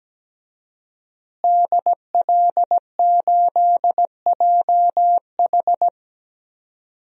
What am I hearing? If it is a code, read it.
Morse code, heard as a keyed tone, about 17 wpm, 700 Hz. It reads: DL8JH
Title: DXCC-100 Call Signs - 5 Characters - Repeated 3x - 17wpm